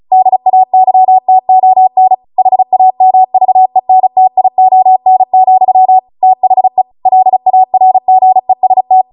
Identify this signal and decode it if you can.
Morse code, heard as a keyed tone, about 36 wpm, 750 Hz. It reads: DAYTON HAMVENTION, THE LARGEST